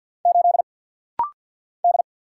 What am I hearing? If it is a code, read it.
Morse code, heard as a keyed tone, about 50 wpm, 700 Hz. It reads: 8 D
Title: Single Letter-Number - Rapid-Fire 50wpm